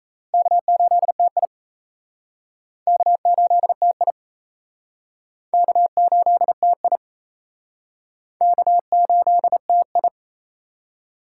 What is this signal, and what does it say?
Morse code, heard as a keyed tone, about 37 wpm, 700 Hz. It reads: K8TI K8TI K8TI K8TI